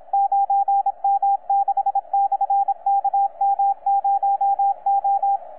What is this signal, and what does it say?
Morse code, heard as a keyed tone, about 27 wpm, 800 Hz, with radio noise. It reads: 9M6/KM0O
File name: 9M6/KM0O 30 CW